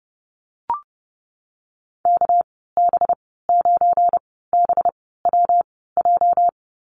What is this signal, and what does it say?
Morse code, heard as a keyed tone, about 30 wpm, spaced as for 17 wpm, 700 Hz. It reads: KB9BWJ
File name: Lesson 37g - Letter x - Call Signs 30wpm